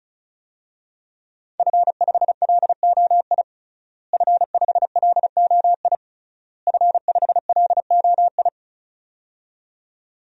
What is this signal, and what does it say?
Morse code, heard as a keyed tone, about 34 wpm, 700 Hz. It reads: F5LOI F5LOI F5LOI
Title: DXCC-100 Call Signs - 5 Characters - Repeated 3x - 35wpm